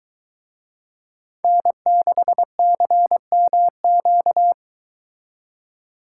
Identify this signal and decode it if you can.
Morse code, heard as a keyed tone, about 23 wpm, 700 Hz. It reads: N6CMQ